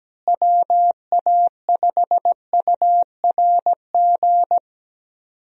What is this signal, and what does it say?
Morse code, heard as a keyed tone, about 17 wpm, 700 Hz. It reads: WA5URG
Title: USA Call Signs - 6 Characters - Repeated 2x - 17wpm